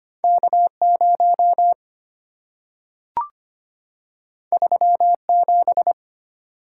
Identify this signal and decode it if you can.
Morse code, heard as a keyed tone, about 25 wpm, 700 Hz. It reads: K0 37
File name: Two Letter-Number Combinations - 25wpm